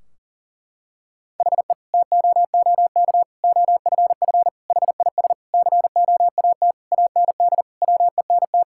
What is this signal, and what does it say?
Morse code, heard as a keyed tone, about 40 wpm, 700 Hz. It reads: HE TOOK OFF HIS COAT AND WENT
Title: Sentences from Top 1600 Words 40wpm